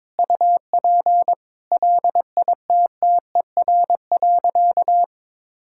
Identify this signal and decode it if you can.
Morse code, heard as a keyed tone, about 21 wpm, 700 Hz. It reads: UP LITTER.
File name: practice / 22wpm / Easy Sentences with Hints - 22wpm